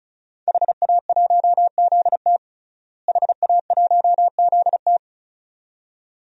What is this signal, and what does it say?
Morse code, heard as a keyed tone, about 36 wpm, 700 Hz. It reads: HA1ZT HA1ZT
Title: DXCC-100 Call Signs - 5 Characters - Repeated 2x - 35wpm